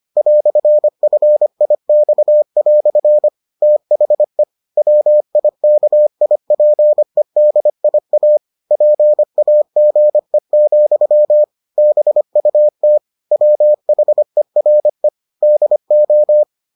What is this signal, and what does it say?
Morse code, heard as a keyed tone, about 25 wpm, 600 Hz. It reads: "FIX" THE WIKIPEDIA PAGE, BUT WHERE DO